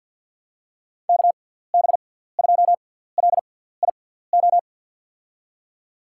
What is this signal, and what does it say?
Morse code, heard as a keyed tone, about 47 wpm, spaced as for 17 wpm, 700 Hz. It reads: KB2LIO
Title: USA Call Signs - 6 Characters - 6x Character Spacing - 50wpm